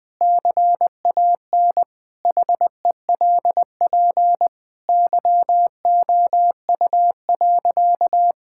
Morse code, heard as a keyed tone, about 20 wpm, 700 Hz. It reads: CAN HELP YOU.
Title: Sentences from Top 1000 Words - Review Part 6 - 20wpm